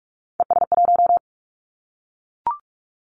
Morse code, heard as a keyed tone, about 44 wpm, 700 Hz. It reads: ES1
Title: Callsign Prefixes for Top 100 DXCC - 45wpm